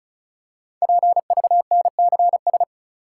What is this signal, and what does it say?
Morse code, heard as a keyed tone, about 36 wpm, 700 Hz. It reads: PVNCS
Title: Code Groups - 5 Letter:Numbers - 35wpm